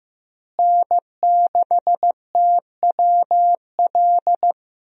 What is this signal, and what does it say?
Morse code, heard as a keyed tone, about 15 wpm, 700 Hz. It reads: N6TWL
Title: USA Call Signs - 5 Characters - Repeated 3x - 15wpm